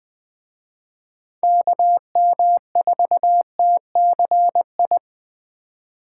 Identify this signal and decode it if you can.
Morse code, heard as a keyed tone, about 24 wpm, 700 Hz. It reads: KM4TCI
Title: Speed Racing - USA 2x3 Call Signs 20wpm